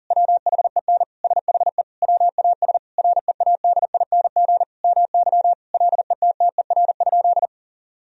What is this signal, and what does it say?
Morse code, heard as a keyed tone, about 40 wpm, 700 Hz. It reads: WHEN SHE WAS READING MY LETTER?
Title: Sentences from Top 1000 Words - Review Part 3 - 40wpm